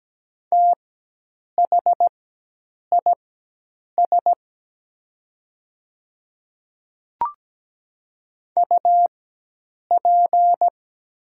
Morse code, heard as a keyed tone, about 17 wpm, spaced as for 8 wpm, 690 Hz. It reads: THIS UP
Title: 4x Character Spacing - Top 300 Words - 17wpm